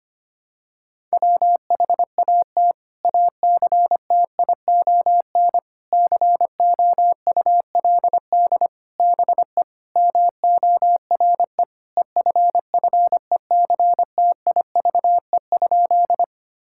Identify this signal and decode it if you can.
Morse code, heard as a keyed tone, about 25 wpm, 700 Hz. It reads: WHAT ACTION COULD BE MORE EFFECTIVE?
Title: Sentences from Top 1000 Words - Review Part 6 - 25wpm